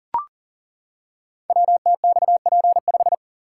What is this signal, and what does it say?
Morse code, heard as a keyed tone, about 42 wpm, 700 Hz. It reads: WTXP5
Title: Code Groups - 5 Letter:Numbers - 40wpm